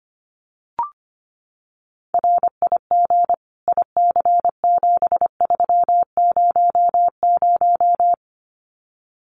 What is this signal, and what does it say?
Morse code, heard as a keyed tone, about 25 wpm, 700 Hz. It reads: RIG IC7300
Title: Common QSO Phrases Encore 25wpm